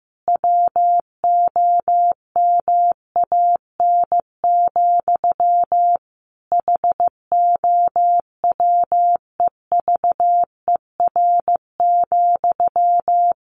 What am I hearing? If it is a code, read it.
Morse code, heard as a keyed tone, about 15 wpm, 700 Hz. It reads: WOMAN, HOWEVER,
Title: Sentences from Top 1100 Words 15wpm